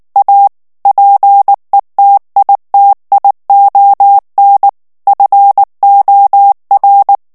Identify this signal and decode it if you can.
Morse code, heard as a keyed tone, about 19 wpm, 800 Hz. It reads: A PETITION FOR
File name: ANGLAIS texte clair 3 vitesse 18 mots minute